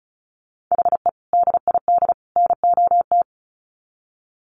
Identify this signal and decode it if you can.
Morse code, heard as a keyed tone, about 34 wpm, 700 Hz. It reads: HE DID NOT
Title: Sets of 3 Words 35wpm